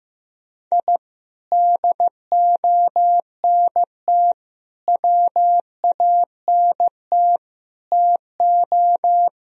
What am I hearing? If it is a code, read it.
Morse code, heard as a keyed tone, about 15 wpm, 700 Hz. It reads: I DONT WANT TO
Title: Sentences from Top 1000 Words - Review Part 1 - 15wpm